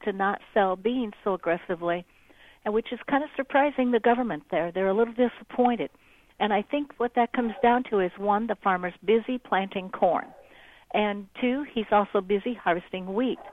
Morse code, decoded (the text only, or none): none